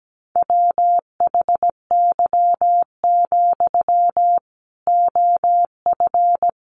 Morse code, heard as a keyed tone, about 17 wpm, 700 Hz. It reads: WHY, OF
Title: Sentences from Top 500 Words 17wpm